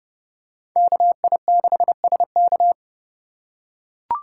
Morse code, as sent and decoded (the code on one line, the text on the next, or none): -.- .. -.... ... -.-
KI6SK